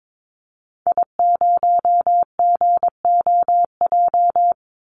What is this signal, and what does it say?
Morse code, heard as a keyed tone, about 22 wpm, 700 Hz. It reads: I0GOJ
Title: Call Signs - 22wpm